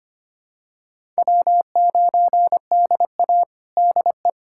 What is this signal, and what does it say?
Morse code, heard as a keyed tone, about 25 wpm, 700 Hz. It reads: W9DA DE
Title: USA Callsign DE Callsign - 4 Characters - 25wpm